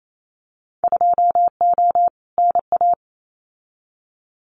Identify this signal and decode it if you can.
Morse code, heard as a keyed tone, about 28 wpm, 700 Hz. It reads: 2O NA
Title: Winter Field Day Exchanges 28wpm